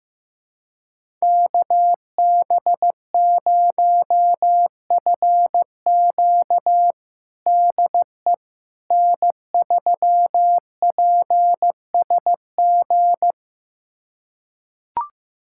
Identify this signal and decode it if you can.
Morse code, heard as a keyed tone, about 15 wpm, 700 Hz. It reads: KB0FQ DE N3PSG
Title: USA Callsign DE Callsign - 5 Characters - 15wpm